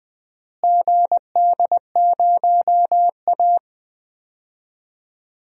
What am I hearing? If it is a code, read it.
Morse code, heard as a keyed tone, about 20 wpm, 700 Hz. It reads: GD0A
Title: DXCC-100 Call Signs - 4 Characters - Repeated 2x - 20wpm